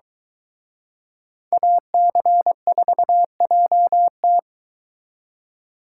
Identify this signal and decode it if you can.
Morse code, heard as a keyed tone, about 22 wpm, 700 Hz. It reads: AC4JT